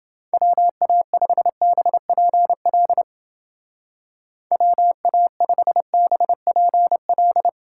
Morse code, heard as a keyed tone, about 30 wpm, 700 Hz. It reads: WA5BPL WA5BPL